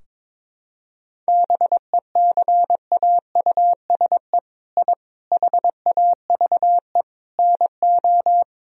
Morse code, heard as a keyed tone, about 22 wpm, 700 Hz. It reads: BECAUSE I HAVE NO
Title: Sentences from Top 1000 Words - Review Part 3 - 22wpm